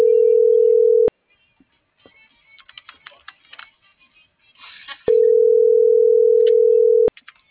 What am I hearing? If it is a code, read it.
Ringback tone (North America), 440 + 480 Hz.
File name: CENTRAL MAYAB / out fernanda.pamela